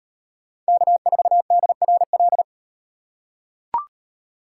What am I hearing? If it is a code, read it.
Morse code, heard as a keyed tone, about 36 wpm, 700 Hz. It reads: K4DRL